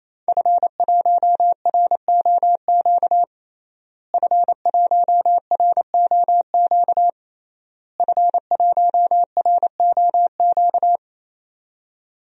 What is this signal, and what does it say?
Morse code, heard as a keyed tone, about 28 wpm, 700 Hz. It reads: F1ROQ F1ROQ F1ROQ
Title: DXCC-100 Call Signs - 5 Characters - Repeated 3x - 28wpm